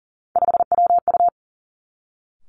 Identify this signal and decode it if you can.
Morse code, heard as a keyed tone, about 42 wpm, 700 Hz. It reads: 5WU
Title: Three Challenging Letter-Number Combinations 40wpm